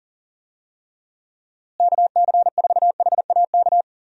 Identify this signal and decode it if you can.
Morse code, heard as a keyed tone, about 40 wpm, 700 Hz. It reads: KC4HAK
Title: USA Call Signs - 6 Characters - Repeated 2x - 40wpm